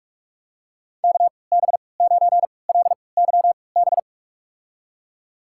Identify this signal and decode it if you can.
Morse code, heard as a keyed tone, about 46 wpm, spaced as for 25 wpm, 700 Hz. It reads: KB9LYB
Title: USA Call Signs - 6 Characters - 3x Character Spacing - 45wpm